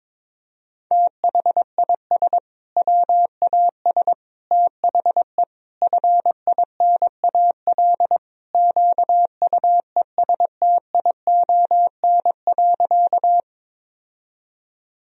Morse code, heard as a keyed tone, about 22 wpm, 700 Hz. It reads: THIS WAS THE FINAL QUESTION.
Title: Sentences from Top 1000 Words - Review Part 2 - 22wpm